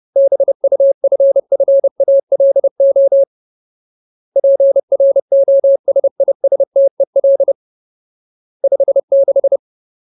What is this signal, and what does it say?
Morse code, heard as a keyed tone, about 30 wpm, 550 Hz. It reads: BUFFALO PROSISTEL 56
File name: Ultimate Word List / 30WPM Ultimate